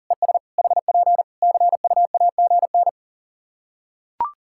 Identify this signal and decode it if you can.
Morse code, heard as a keyed tone, about 40 wpm, 700 Hz. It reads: ES HP CUAGN
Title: Practice QSO - Exchange 4 - Ending - 40wpm